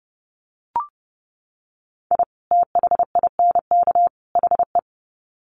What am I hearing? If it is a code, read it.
Morse code, heard as a keyed tone, about 30 wpm, 700 Hz. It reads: I THINK HE
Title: Sets of 3 Words 30wpm